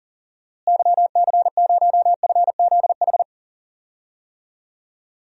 Morse code, heard as a keyed tone, about 37 wpm, 700 Hz. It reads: YC0FZH
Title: DXCC-100 Call Signs - 6 Characters - Repeated 2x - 40wpm